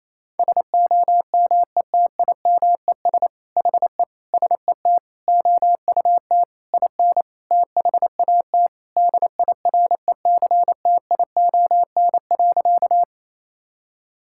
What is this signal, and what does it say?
Morse code, heard as a keyed tone, about 28 wpm, 700 Hz. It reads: SOMETIMES HE SET OUT IN THAT DIRECTION.
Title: Sentences from Top 1000 Words - Review Part 5 - 28wpm